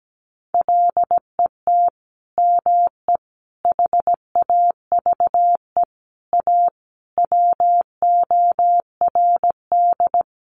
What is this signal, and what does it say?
Morse code, heard as a keyed tone, about 17 wpm, 700 Hz. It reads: LET ME HAVE A WORD
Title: Sentences from Top 300 Words 17wpm